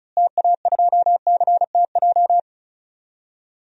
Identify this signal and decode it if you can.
Morse code, heard as a keyed tone, about 34 wpm, 700 Hz. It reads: TA2CTJ
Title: DXCC-100 Call Signs - 6 Characters - Repeated 2x - 35wpm